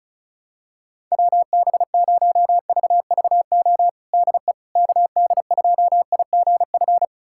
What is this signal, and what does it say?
Morse code, heard as a keyed tone, about 36 wpm, 700 Hz. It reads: WB0VVO DE KD2IGF
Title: USA Callsign DE Callsign - 6 Characters - 35wpm